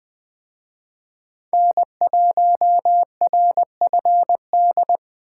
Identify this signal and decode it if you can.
Morse code, heard as a keyed tone, about 24 wpm, 700 Hz. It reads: N1RFD
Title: Speed Racing - USA Call Signs 20wpm